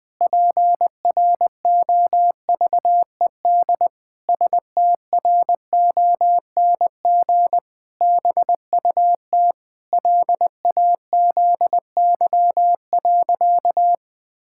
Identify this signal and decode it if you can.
Morse code, heard as a keyed tone, about 21 wpm, 700 Hz. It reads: PROVED STRONG BUT LAZY.
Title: Pangram Challenge v2 - 20wpm